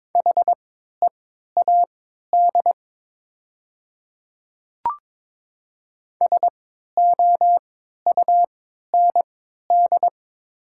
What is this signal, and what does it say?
Morse code, heard as a keyed tone, about 22 wpm, spaced as for 12 wpm, 700 Hz. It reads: HEAD SOUND
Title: 3x Character Spacing - Top 300 Words - 22wpm